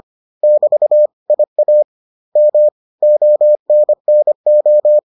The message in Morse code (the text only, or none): <BT> IAM ONNO